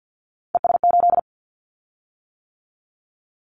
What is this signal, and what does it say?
Morse code, heard as a keyed tone, about 50 wpm, 700 Hz. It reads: ES8
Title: Callsign Prefixes for Top 100 DXCC - 50wpm